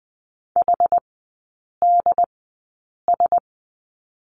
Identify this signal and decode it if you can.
Morse code, heard as a keyed tone, about 19 wpm, spaced as for 8 wpm, 700 Hz. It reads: HDS
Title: 067 Lesson 12e - Letter d - Three Characters 20wpm